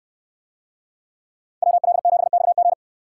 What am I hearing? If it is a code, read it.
Morse code, heard as a keyed tone, about 69 wpm, 700 Hz. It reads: WL7XG